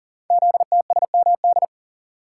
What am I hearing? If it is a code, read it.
Morse code, heard as a keyed tone, about 42 wpm, 690 Hz. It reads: ZTSMD
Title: Code Groups - 5 Letters - 40wpm